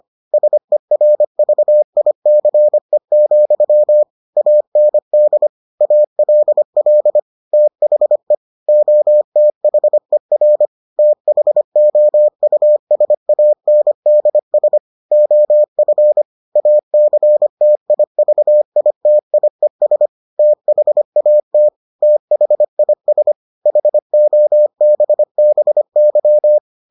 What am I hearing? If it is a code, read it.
Morse code, heard as a keyed tone, about 25 wpm, 600 Hz. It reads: SERVICE, AND ALL THE OTHER THOUSANDS OF ACTIVITIES THAT THIS HOBBY